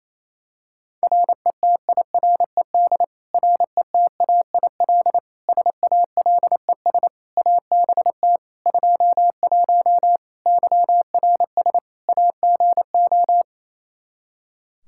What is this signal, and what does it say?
Morse code, heard as a keyed tone, about 28 wpm, 700 Hz. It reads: RETIRED RETAIL SALES ABT 21 YRS AGO
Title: Practice QSO - Exchange 3 - More Info - 28wpm